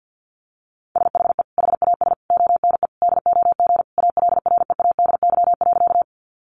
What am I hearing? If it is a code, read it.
Morse code, heard as a keyed tone, about 50 wpm, 700 Hz. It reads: SHE HAS ONE DOG ALREADY.
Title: Easy Phrases 50wpm